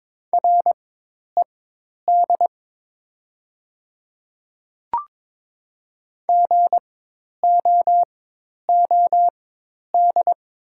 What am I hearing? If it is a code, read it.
Morse code, heard as a keyed tone, about 21 wpm, spaced as for 10 wpm, 700 Hz. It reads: RED GOOD
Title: 4x Character Spacing - Top 500 Words - 22wpm